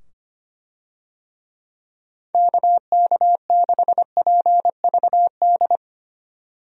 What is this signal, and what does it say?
Morse code, heard as a keyed tone, about 25 wpm, 700 Hz. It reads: KK6PVD
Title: USA Call Signs - 6 Characters - 25wpm